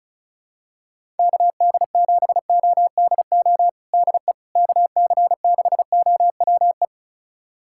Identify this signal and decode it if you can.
Morse code, heard as a keyed tone, about 36 wpm, 690 Hz. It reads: KD7ODO DE KC6OWE
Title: USA Callsign DE Callsign - 6 Characters - 35wpm